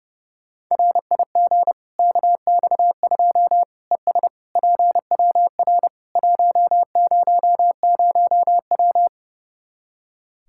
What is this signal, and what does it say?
Morse code, heard as a keyed tone, about 32 wpm, 700 Hz. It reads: RIG KX2 ES PWR 100W
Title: K1USN SST QSO Practice 30wpm